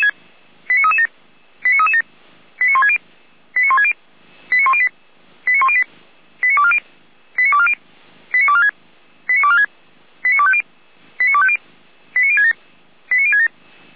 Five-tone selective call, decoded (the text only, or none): 89337 89337 89160 89160 89108 89108 89240 89240 89246 89246 89250 89250 89067 89067